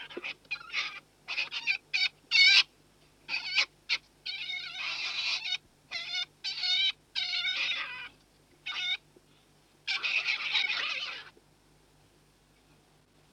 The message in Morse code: none